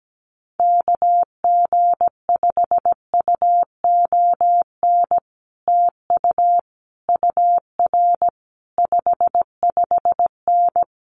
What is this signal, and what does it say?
Morse code, heard as a keyed tone, about 17 wpm, 700 Hz. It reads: KG5UON TU UR 55N
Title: Common QSO Phrases Encore 17wpm